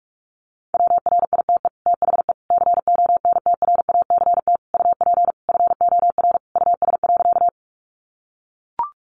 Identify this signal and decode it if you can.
Morse code, heard as a keyed tone, about 44 wpm, 700 Hz. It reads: WRITE THE CONTRACT UP FOR US.
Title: Sentences from Top 1400 Words 45wpm